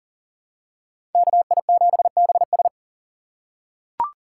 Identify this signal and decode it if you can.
Morse code, heard as a keyed tone, about 40 wpm, 700 Hz. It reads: KI7BS
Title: USA Call Signs - 5 Characters - 2x Character Spacing - 40wpm